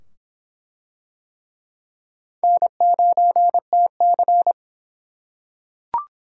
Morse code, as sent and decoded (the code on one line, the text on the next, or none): -. ----. - -.-.
N9TC